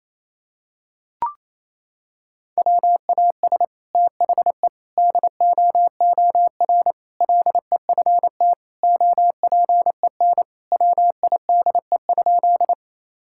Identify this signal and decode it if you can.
Morse code, heard as a keyed tone, about 28 wpm, 700 Hz. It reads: WAS THE DOOR LEFT OPEN WIDE?
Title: Sentences from Top 1000 Words - Review Part 6 - 28wpm